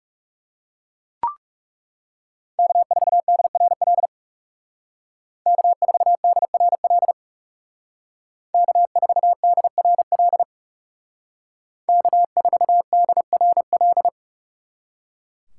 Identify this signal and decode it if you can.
Morse code, heard as a keyed tone, about 40 wpm, 700 Hz. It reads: K4DRL K4DRL K4DRL K4DRL